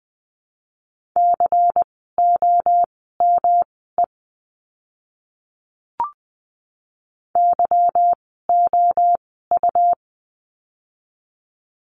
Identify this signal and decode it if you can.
Morse code, heard as a keyed tone, about 20 wpm, spaced as for 14 wpm, 700 Hz. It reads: COME YOU
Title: 206 Lesson 33f - Letter j - Words 20wpm